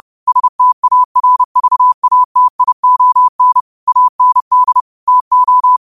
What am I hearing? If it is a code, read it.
Morse code, heard as a keyed tone, about 30 wpm, 1000 Hz. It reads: STARVATION AND TO
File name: qotdfiles / 30wpm